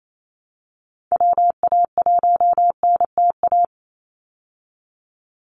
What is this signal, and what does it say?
Morse code, heard as a keyed tone, about 28 wpm, 700 Hz. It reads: WA1NTA
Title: Call Signs Repeated 3x - 28wpm